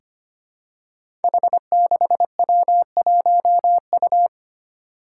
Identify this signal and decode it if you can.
Morse code, heard as a keyed tone, about 25 wpm, 700 Hz. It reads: H6W1U
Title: Five Challenging Letter-Number Combinations 25wpm